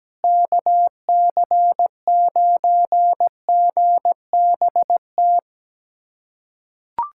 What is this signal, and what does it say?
Morse code, heard as a keyed tone, about 17 wpm, 700 Hz. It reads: KC9GBT
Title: USA Callsign DE Callsign - 6 Characters - 17wpm